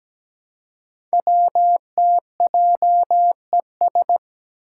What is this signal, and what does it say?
Morse code, heard as a keyed tone, about 17 wpm, 700 Hz. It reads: WTJES
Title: Code Groups - 5 Letters - 17wpm